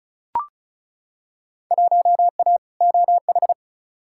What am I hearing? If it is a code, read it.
Morse code, heard as a keyed tone, about 34 wpm, 700 Hz. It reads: 1A OH
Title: Field Day Exchanges - 35wpm